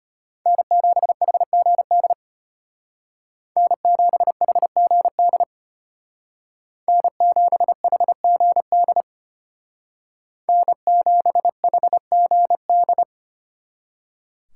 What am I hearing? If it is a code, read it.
Morse code, heard as a keyed tone, about 32 wpm, 700 Hz. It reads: N7HGD N7HGD N7HGD N7HGD